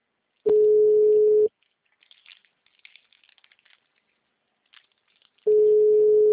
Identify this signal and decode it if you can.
Ringback tone (Europe, 425 Hz).
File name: out- -2004- - - .22